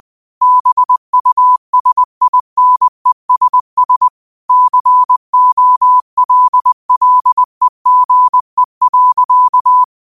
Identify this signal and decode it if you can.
Morse code, heard as a keyed tone, about 20 wpm, 1000 Hz. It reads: BUSINESS COLLEGE.
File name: qotdfiles / 20wpm